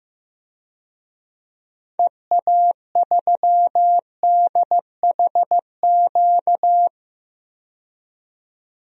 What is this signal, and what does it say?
Morse code, heard as a keyed tone, about 15 wpm, 700 Hz. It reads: EA3DHQ
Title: DXCC-100 Call Signs - 6 Characters - 15wpm